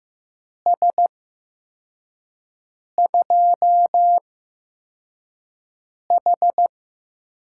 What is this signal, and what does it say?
Morse code, heard as a keyed tone, about 15 wpm, spaced as for 4 wpm, 700 Hz. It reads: S2H
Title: USA Call Signs - 8x Character Spacing - 15wpm